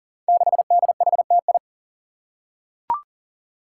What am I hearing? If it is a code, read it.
Morse code, heard as a keyed tone, about 40 wpm, 700 Hz. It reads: 6DHTI
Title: Five Challenging Letter-Number Combinations 40wpm